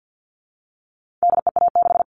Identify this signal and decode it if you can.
Morse code, heard as a keyed tone, about 50 wpm, 700 Hz. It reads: DEA6
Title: All Cut Numbers - 4-Digit 50wpm